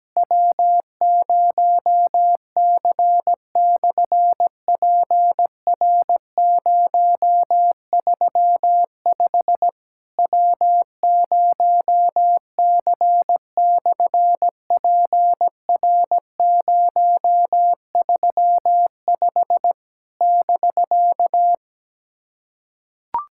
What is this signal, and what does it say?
Morse code, heard as a keyed tone, about 17 wpm, 700 Hz. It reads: W0C/PR035 W0C/PR035 <BK>
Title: SOTA QSO Practice 17wpm